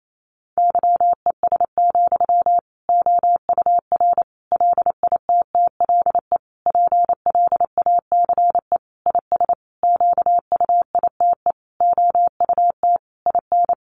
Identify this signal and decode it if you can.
Morse code, heard as a keyed tone, about 28 wpm, 700 Hz. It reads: YES, OUR LITTLE PLACE IS QUITE OUT IN
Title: Sentences from Top 700 Words 28wpm